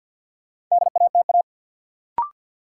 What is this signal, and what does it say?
Morse code, heard as a keyed tone, about 53 wpm, 700 Hz. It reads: DATA
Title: Sentences from Top 1000 Words - Review Part 3 - New Word Drill - 50wpm